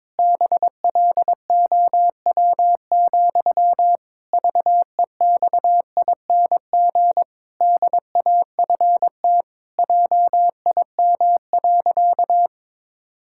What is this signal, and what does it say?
Morse code, heard as a keyed tone, about 22 wpm, 700 Hz. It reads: BLOW, VEXING DAFT JIM.
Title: Pangram Challenge v2 - 22wpm